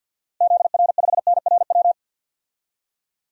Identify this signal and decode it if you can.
Morse code, heard as a keyed tone, about 52 wpm, 700 Hz. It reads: ZR5NRW
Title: DXCC-100 Call Signs - 6 Characters - Repeated 3x - 50wpm